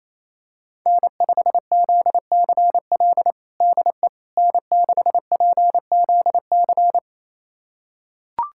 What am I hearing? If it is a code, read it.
Morse code, heard as a keyed tone, about 28 wpm, 700 Hz. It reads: N5ZCL DE N6PZC
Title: USA Callsign DE Callsign 28wpm